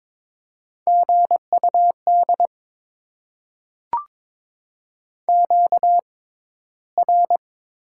Morse code, heard as a keyed tone, about 22 wpm, spaced as for 13 wpm, 690 Hz. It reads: GUD QR
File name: Top 20 QSO Elements - 6x Character Spacing - 22wpm